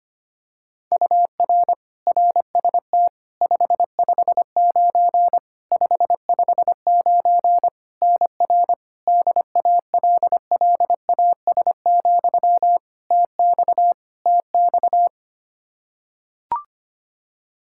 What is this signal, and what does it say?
Morse code, heard as a keyed tone, about 25 wpm, 700 Hz. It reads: UR RST 559 559 NR DALLAS, TX TX
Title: Practice QSO - Exchange 1 - Beginning - 25wpm